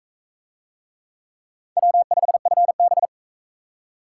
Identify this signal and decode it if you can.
Morse code, heard as a keyed tone, about 38 wpm, 700 Hz. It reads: W5FB